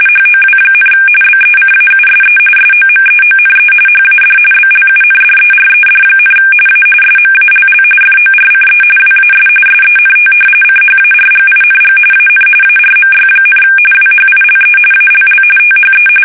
RTTY, about 45.45 baud. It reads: HOW, BECAUSE SOMEONE HAD BLUNDERED, A SMALL FORCE WAS SENT CHARGING AN ARMY, WHILE ALL THE
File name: fsk850 / battleshipnj